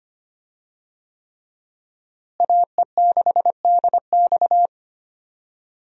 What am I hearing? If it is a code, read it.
Morse code, heard as a keyed tone, about 25 wpm, 700 Hz. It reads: AE6DX
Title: USA Call Signs - 5 Characters - Repeated 3x - 25wpm